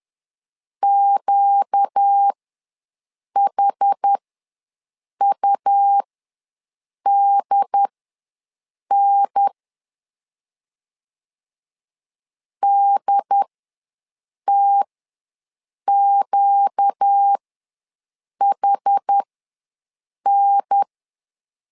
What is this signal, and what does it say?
Morse code, heard as a keyed tone, about 10 wpm, spaced as for 6 wpm, 800 Hz. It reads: QHUDN DTQHN
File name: LEZIONI / 4 Q ws 10wpm 2306